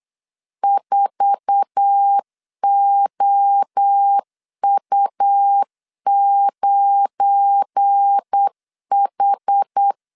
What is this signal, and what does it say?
Morse code, heard as a keyed tone, about 8 wpm, 800 Hz. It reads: 4OU9H
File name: LEZIONI / 8 YJ45 / 8 YJ4 8wpm 3246